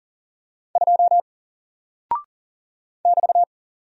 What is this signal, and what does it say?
Morse code, heard as a keyed tone, about 40 wpm, 700 Hz. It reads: 2 <BT>
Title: Single Character (Letter:Number:Punctuation:Prosign) 40wpm